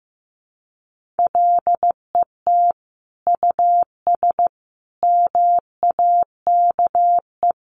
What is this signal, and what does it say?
Morse code, heard as a keyed tone, about 15 wpm, 700 Hz. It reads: LET US MAKE
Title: Sentences from Top 1500 Words 15wpm